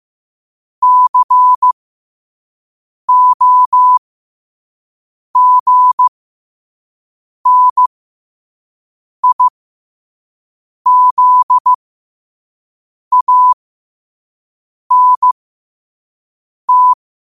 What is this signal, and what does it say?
Morse code, heard as a keyed tone, about 15 wpm, spaced as for 5 wpm, 1000 Hz. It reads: COGNIZANT
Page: New quotes every day in morse code at 5 Words per minute.